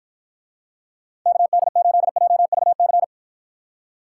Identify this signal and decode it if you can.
Morse code, heard as a keyed tone, about 53 wpm, 700 Hz. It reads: KD8JVC